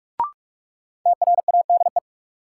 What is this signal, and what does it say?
Morse code, heard as a keyed tone, about 46 wpm, 700 Hz. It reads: TRADE
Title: Sentences from Top 600 Words - New Word Drill - 45wpm